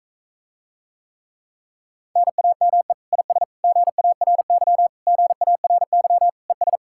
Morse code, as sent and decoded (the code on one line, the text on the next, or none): -. .- -- . / .. ... / --. .- .-. -.-- / --. .- .-. -.-- / . ...
NAME IS GARY GARY ES